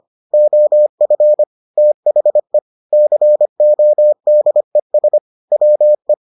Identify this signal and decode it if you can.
Morse code, heard as a keyed tone, about 25 wpm, 600 Hz. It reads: OF THE CODES WE